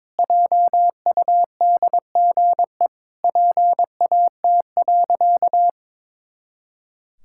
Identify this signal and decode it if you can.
Morse code, heard as a keyed tone, about 22 wpm, 700 Hz. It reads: JUDGE PAT.
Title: Pangram Challenge v2 - 22wpm